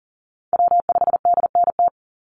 Morse code, heard as a keyed tone, about 42 wpm, 700 Hz. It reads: W5DNT
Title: Call Signs - 40wpm